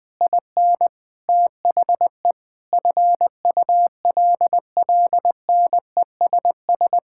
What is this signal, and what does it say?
Morse code, heard as a keyed tone, about 19 wpm, 700 Hz. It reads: IN THE FULLNESS
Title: Sets of 9 Words - 20wpm